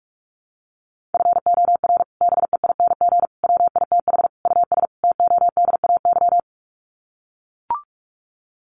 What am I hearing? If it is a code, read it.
Morse code, heard as a keyed tone, about 44 wpm, 700 Hz. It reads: FOR BEING WITH US TODAY
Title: Sets of 5 Words Encore2 45wpm